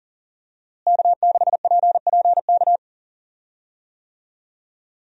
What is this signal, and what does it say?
Morse code, heard as a keyed tone, about 40 wpm, 700 Hz. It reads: K6PPK
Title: USA Call Signs - 5 Characters - Repeated 3x - 40wpm